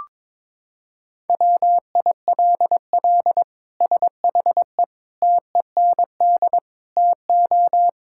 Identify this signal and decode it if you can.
Morse code, heard as a keyed tone, about 22 wpm, 700 Hz. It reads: WILL SHE TEND TO
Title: Sentences from Top 1000 Words - Review Part 5 - 22wpm